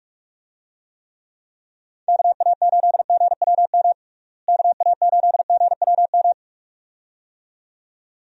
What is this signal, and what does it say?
Morse code, heard as a keyed tone, about 46 wpm, 700 Hz. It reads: KA8GWM KA8GWM
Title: USA Call Signs - 6 Characters - Repeated 2x - 45wpm